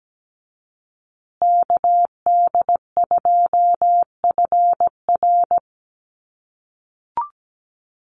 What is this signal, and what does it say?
Morse code, heard as a keyed tone, about 17 wpm, 700 Hz. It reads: KD2FR
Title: USA Call Signs Repeated 2x 17wpm